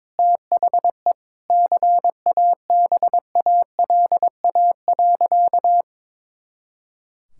Morse code, heard as a keyed tone, about 22 wpm, 700 Hz. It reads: THE CABALA.
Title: Pangram Challenge v2 - 22wpm